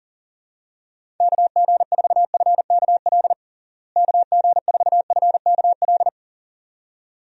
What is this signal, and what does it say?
Morse code, heard as a keyed tone, about 40 wpm, 700 Hz. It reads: KG4FKL KG4FKL
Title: USA Call Signs - 6 Characters - Repeated 2x - 40wpm